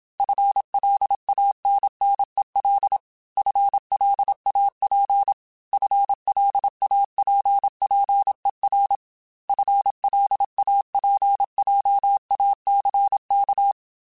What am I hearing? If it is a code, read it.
Morse code, heard as a keyed tone, about 27 wpm, 800 Hz. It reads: FLANNEL FLAP FLAPPER FLAPJACK